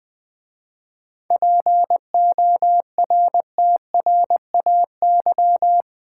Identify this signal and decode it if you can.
Morse code, heard as a keyed tone, about 20 wpm, 690 Hz. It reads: PORTRAY
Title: 7 Letter Words - First 3 Letters Given - 20wpm